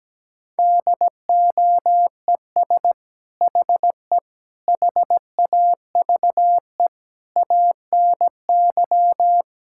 Morse code, heard as a keyed tone, about 17 wpm, 700 Hz. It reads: DOES HE HAVE ANY
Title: Sentences from Top 1600 Words 17wpm